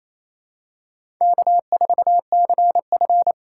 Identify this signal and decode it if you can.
Morse code, heard as a keyed tone, about 28 wpm, 700 Hz. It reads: K4CF
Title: USA Call Signs - 4 Characters - 2x Character Spacing - 28wpm